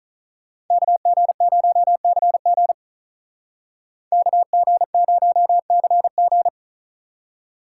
Morse code, heard as a keyed tone, about 46 wpm, 700 Hz. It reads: KG0CG KG0CG